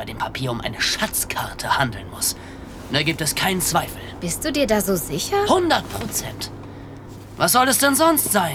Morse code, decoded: none